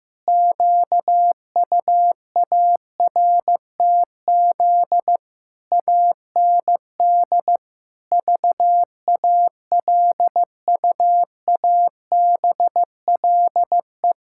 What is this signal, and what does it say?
Morse code, heard as a keyed tone, about 15 wpm, 700 Hz. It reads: QUARTZ AND VALUABLE
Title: Pangram Challenge v2 - 15wpm